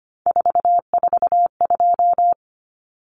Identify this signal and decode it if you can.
Morse code, heard as a keyed tone, about 25 wpm, 700 Hz. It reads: 442
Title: Numbers - 3-Digit 25wpm